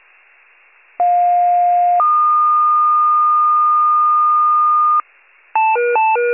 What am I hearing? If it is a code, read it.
Two-tone page, heard as a one-second tone, then a three-second tone, about 710 then 1220 Hz. It is heard from 1.0 to 5.0 s.